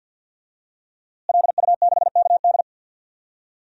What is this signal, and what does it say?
Morse code, heard as a keyed tone, about 52 wpm, 700 Hz. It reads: LU6KD
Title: DXCC-100 Call Signs - 5 Characters - Repeated 2x - 50wpm